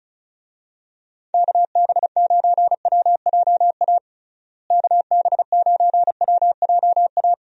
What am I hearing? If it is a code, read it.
Morse code, heard as a keyed tone, about 36 wpm, 700 Hz. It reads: KB9WJA KB9WJA
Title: USA Call Signs - 6 Characters - Repeated 3x - 35wpm